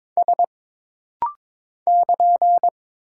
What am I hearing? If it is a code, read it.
Morse code, heard as a keyed tone, about 22 wpm, 700 Hz. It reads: S <KN>
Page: This rapid-fire timing is geared to push your ICR (Instant Character Recognition) skills.